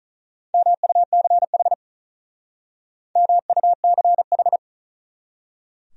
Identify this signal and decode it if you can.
Morse code, heard as a keyed tone, about 45 wpm, 700 Hz. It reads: MUCH MUCH